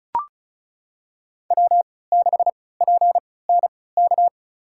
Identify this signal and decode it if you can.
Morse code, heard as a keyed tone, about 36 wpm, spaced as for 20 wpm, 700 Hz. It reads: W6PNK
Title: USA Call Signs - 3x Character Spacing - 35wpm